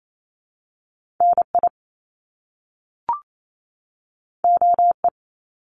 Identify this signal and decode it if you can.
Morse code, heard as a keyed tone, about 28 wpm, 700 Hz. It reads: NI OE
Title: Top Two Letter Combinations in Words 28wpm